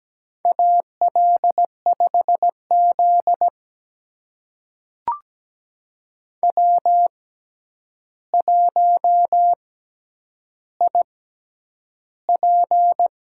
Morse code, heard as a keyed tone, about 17 wpm, spaced as for 9 wpm, 700 Hz. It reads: AL5Z W1IP
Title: USA Call Signs - 4 Characters - 6x Character Spacing - 17wpm